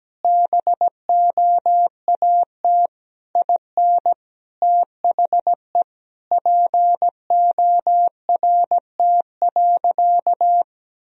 Morse code, heard as a keyed tone, about 17 wpm, 700 Hz. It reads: BOAT IN THE PORT.
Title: Sentences from Top 1000 Words - Review Part 1 - 17wpm